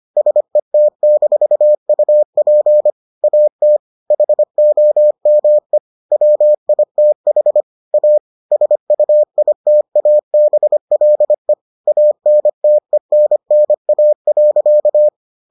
Morse code, heard as a keyed tone, about 25 wpm, 600 Hz. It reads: SET-UP AT HOME WITH A SUITABLE ANTENNA.